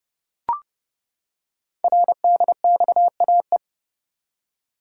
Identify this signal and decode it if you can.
Morse code, heard as a keyed tone, about 30 wpm, 700 Hz. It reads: RDXAE
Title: Code Groups - 5 Letters - 30wpm